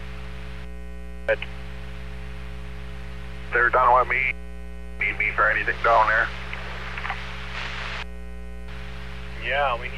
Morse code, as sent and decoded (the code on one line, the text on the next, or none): none